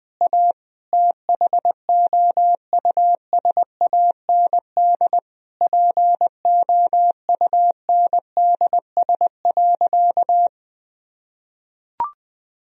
Morse code, heard as a keyed tone, about 20 wpm, 700 Hz. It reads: A THOUSAND POUNDS.
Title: Sentences from Top 1000 Words - Review Part 3 - 20wpm